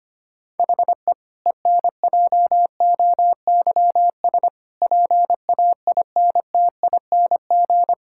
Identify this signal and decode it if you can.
Morse code, heard as a keyed tone, about 25 wpm, 700 Hz. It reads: HE ENJOYS PAINTING
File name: practice / 25wpm / Difficult Sentences with Hints - 25wpm